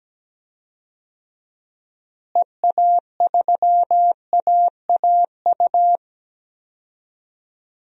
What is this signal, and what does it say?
Morse code, heard as a keyed tone, about 17 wpm, 700 Hz. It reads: EA3AAU
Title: DXCC-100 Call Signs - 6 Characters - Repeated 3x - 17wpm